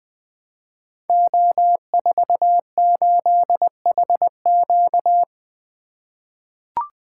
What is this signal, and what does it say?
Morse code, heard as a keyed tone, about 20 wpm, 700 Hz. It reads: O48HQ
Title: Code Groups - 5 Letter:Numbers - 20wpm